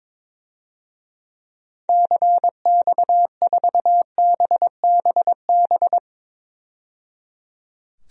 Morse code, heard as a keyed tone, about 22 wpm, 700 Hz. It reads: CX4BBB
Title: DXCC-100 Call Signs - 6 Characters - 22wpm